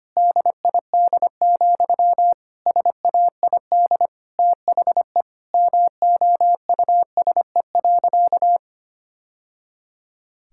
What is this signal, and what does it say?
Morse code, heard as a keyed tone, about 25 wpm, 700 Hz. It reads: DID, SAID THE MOUSE.
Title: Sentences from Top 1000 Words - Review Part 5 - 25wpm